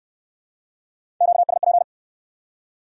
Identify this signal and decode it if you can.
Morse code, heard as a keyed tone, about 62 wpm, 700 Hz. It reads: QSP